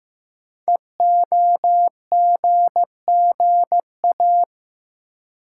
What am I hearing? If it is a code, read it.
Morse code, heard as a keyed tone, about 15 wpm, 700 Hz. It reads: EOGGA
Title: Code Groups - 5 Letters - 15wpm